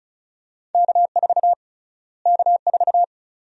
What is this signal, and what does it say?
Morse code, heard as a keyed tone, about 36 wpm, 700 Hz. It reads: K4 K4
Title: USA Call Sign Prefixes - Repeated 2x - 35wpm